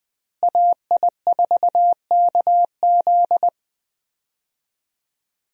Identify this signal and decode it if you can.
Morse code, heard as a keyed tone, about 20 wpm, 700 Hz. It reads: AI4KZ
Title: USA Call Signs - 5 Characters - Repeated 3x - 20wpm